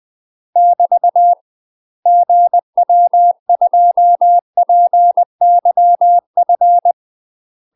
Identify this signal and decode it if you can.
Morse code, heard as a keyed tone, about 20 wpm, 700 Hz. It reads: <BT> GW2PYF
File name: UK MASTER QSOs TRAINING / 20WPM / UK MASTER QSOs 0037